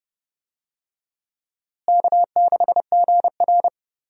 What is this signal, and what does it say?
Morse code, heard as a keyed tone, about 30 wpm, 700 Hz. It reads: K6GR
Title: USA Call Signs - 4 Characters - Repeated 3x - 30wpm